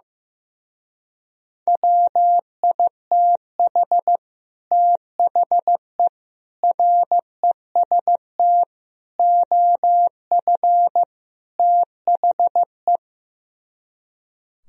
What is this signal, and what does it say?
Morse code, heard as a keyed tone, about 16 wpm, 700 Hz. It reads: WITH THE REST OF THE
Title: Speed-Racing - Sets of 5 Words 15wpm